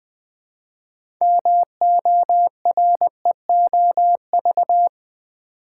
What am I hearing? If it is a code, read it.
Morse code, heard as a keyed tone, about 20 wpm, 700 Hz. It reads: MOREOV
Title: Build A Word - 2 Letters At A Time - 8-Letter Words - 20wpm